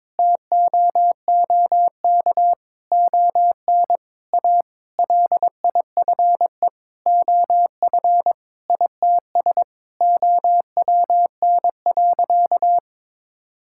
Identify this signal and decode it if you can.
Morse code, heard as a keyed tone, about 22 wpm, 700 Hz. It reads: TOOK ON A LIFE OF ITS OWN.
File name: Sentences with 7-Word Phrases - 22wpm